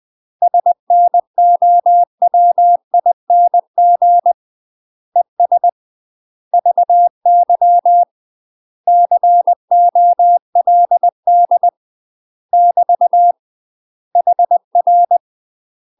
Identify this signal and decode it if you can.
Morse code, heard as a keyed tone, about 20 wpm, 700 Hz. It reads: SNOWING ES VY COLD <BT> HR
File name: UK MASTER QSOs TRAINING / 20WPM / UK MASTER QSOs 0063